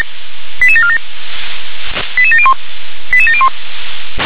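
Five-tone selective call (ZVEI), heard as quick runs of five tones, tones of about 70 ms. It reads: 88747 88701 88701
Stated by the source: ZVEI five-tone call received by SDR.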